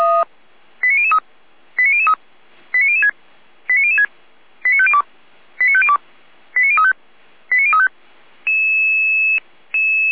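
Five-tone selective call (ZVEI), heard as quick runs of five tones, tones of about 70 ms. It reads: 89003 89003 89006 89006 89502 89502 89035 89035
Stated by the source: ZVEI five-tone call received on an SDR, recorded by the Feuerwehr Orlamünde fire brigade.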